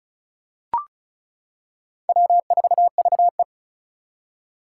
Morse code, heard as a keyed tone, about 36 wpm, 700 Hz. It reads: W4VE
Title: Four Challenging Letter-Number Combinations 35wpm